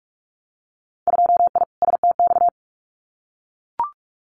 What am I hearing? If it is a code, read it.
Morse code, heard as a keyed tone, about 44 wpm, 700 Hz. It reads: 2I STX
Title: Winter Field Day Exchanges 45wpm